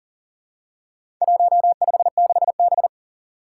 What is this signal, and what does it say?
Morse code, heard as a keyed tone, about 40 wpm, 700 Hz. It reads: 156B
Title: Four Challenging Letter-Number Combinations 40wpm